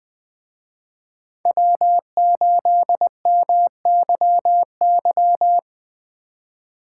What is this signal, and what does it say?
Morse code, heard as a keyed tone, about 20 wpm, 700 Hz. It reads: W8MYY